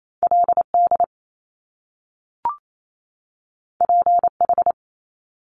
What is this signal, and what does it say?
Morse code, heard as a keyed tone, about 28 wpm, 700 Hz. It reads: LD PH
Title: Top Two Letter Combinations in Words 28wpm